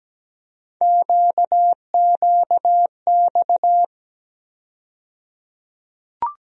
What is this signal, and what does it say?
Morse code, heard as a keyed tone, about 17 wpm, 700 Hz. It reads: QQX
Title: USA Call Sign Postfixes - 3 Letters 17wpm